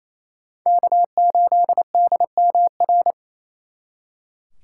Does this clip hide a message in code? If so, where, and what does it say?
Yes: Morse code from 0.7 to 3.1 s, heard as a keyed tone, about 36 wpm, 700 Hz. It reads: K8DMR